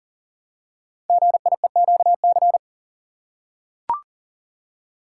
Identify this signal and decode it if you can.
Morse code, heard as a keyed tone, about 43 wpm, 700 Hz. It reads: GIEQC
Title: Code Groups - 5 Letter:Numbers - 40wpm